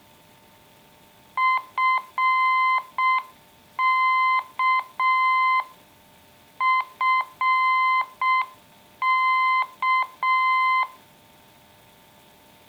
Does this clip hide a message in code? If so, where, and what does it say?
Yes: Morse code from 1.4 to 10.8 s, heard as a keyed tone, about 6 wpm, 1040 Hz. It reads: FK FK